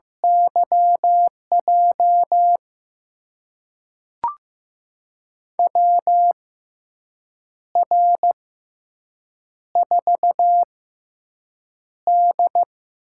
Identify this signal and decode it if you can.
Morse code, heard as a keyed tone, about 15 wpm, spaced as for 6 wpm, 700 Hz. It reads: YJ WR4D
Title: USA Call Signs - 4 Characters - 6x Character Spacing - 15wpm